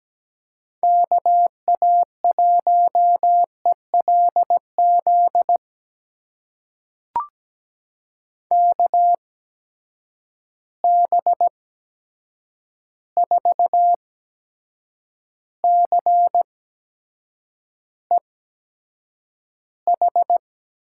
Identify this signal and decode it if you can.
Morse code, heard as a keyed tone, about 17 wpm, spaced as for 7 wpm, 700 Hz. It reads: KA1ELZ KB4CEH
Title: USA Call Signs - 8x Character Spacing - 17wpm